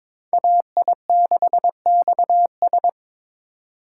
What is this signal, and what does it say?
Morse code, heard as a keyed tone, about 26 wpm, 700 Hz. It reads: AI6XS
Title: Speed Racing - USA 2x2 Call Signs 22wpm